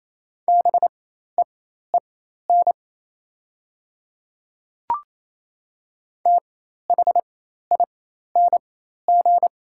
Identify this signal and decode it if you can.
Morse code, heard as a keyed tone, about 28 wpm, spaced as for 13 wpm, 700 Hz. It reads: BEEN THING
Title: 4x Character Spacing - Top 300 Words - 28wpm